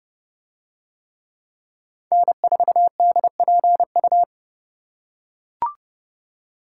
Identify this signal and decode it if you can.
Morse code, heard as a keyed tone, about 28 wpm, 700 Hz. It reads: N4DPU